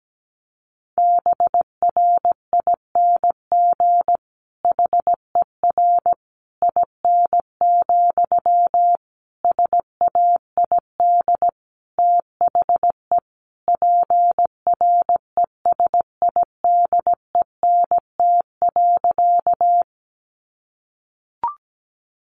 Morse code, heard as a keyed tone, about 17 wpm, 700 Hz. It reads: BRING HER IN, SAID THE PRESIDENT.
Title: Sentences from Top 1000 Words 17wpm